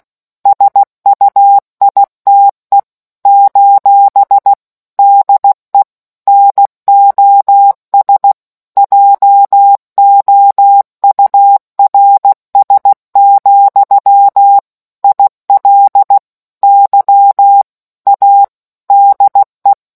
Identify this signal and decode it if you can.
Morse code, heard as a keyed tone, about 16 wpm, 800 Hz. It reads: SUITE : DE NOS JOURS, IL Y A DE